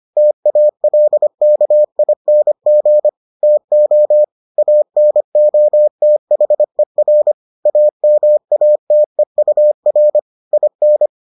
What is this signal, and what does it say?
Morse code, heard as a keyed tone, about 25 wpm, 600 Hz. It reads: TALKING TO ANOTHER AMATEUR IN